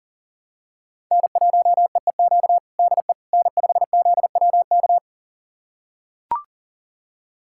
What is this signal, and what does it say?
Morse code, heard as a keyed tone, about 40 wpm, 700 Hz. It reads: N1EEQ DE N5ZWK
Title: USA Callsign DE Callsign - 5 Characters - 40wpm